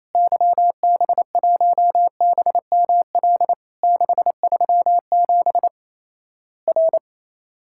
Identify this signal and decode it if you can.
Morse code, heard as a keyed tone, about 27 wpm, 700 Hz. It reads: YB1BML 637 R
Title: Worked All Europe DX Contest - QTC Exchange - 28wpm